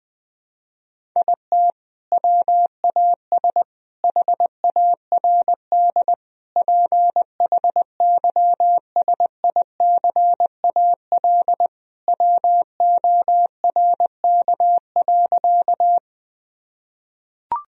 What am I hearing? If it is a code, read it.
Morse code, heard as a keyed tone, about 21 wpm, 700 Hz. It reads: IT WAS HARD PHYSICAL WORK.
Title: Sentences from Top 1000 Words - Review Part 4 - 20wpm